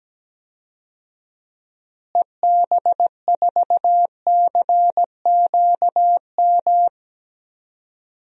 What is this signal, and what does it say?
Morse code, heard as a keyed tone, about 17 wpm, 700 Hz. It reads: EB4CQM
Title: DXCC-100 Call Signs - 6 Characters - Repeated 3x - 17wpm